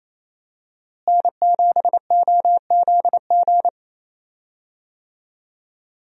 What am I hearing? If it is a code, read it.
Morse code, heard as a keyed tone, about 27 wpm, 700 Hz. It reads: N7OZG
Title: USA Call Signs - 5 Characters - Repeated 3x - 28wpm